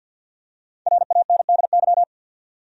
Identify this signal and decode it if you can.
Morse code, heard as a keyed tone, about 50 wpm, 700 Hz. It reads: RANDY
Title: USA Common Names - Max 5 Letters - 50wpm